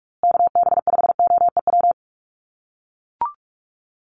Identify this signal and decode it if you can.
Morse code, heard as a keyed tone, about 44 wpm, 690 Hz. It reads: KB5OEW
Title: USA Call Signs Repeated 3x 45wpm